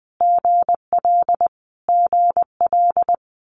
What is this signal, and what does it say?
Morse code, heard as a keyed tone, about 20 wpm, 700 Hz. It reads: GL GL
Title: Multi-letter Phonemes Repeated 3x 20wpm